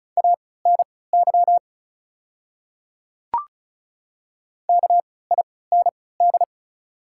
Morse code, heard as a keyed tone, about 36 wpm, spaced as for 20 wpm, 690 Hz. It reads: ANY KIND
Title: 3x Character Spacing - Top 200 Words - 35wpm